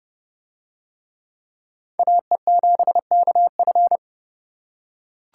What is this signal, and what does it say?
Morse code, heard as a keyed tone, about 30 wpm, 700 Hz. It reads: AE7KF